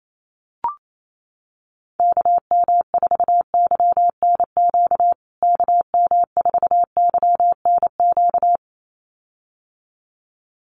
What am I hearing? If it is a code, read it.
Morse code, heard as a keyed tone, about 28 wpm, 700 Hz. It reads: KM4YNQ KM4YNQ
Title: USA Call Signs Repeated 2x 28wpm